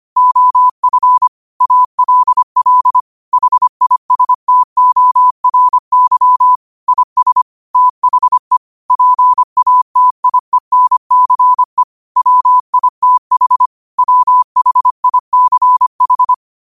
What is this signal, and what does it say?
Morse code, heard as a keyed tone, about 25 wpm, 1000 Hz. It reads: OF ALL HISTORY IS THE PATIENCE WITH WHICH
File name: qotdfiles / 25wpm